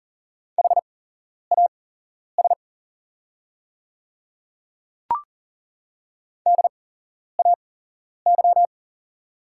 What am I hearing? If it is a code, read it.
Morse code, heard as a keyed tone, about 40 wpm, spaced as for 11 wpm, 700 Hz. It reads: HAS DAY
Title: 8x Character Spacing - Top 100 Words - 40wpm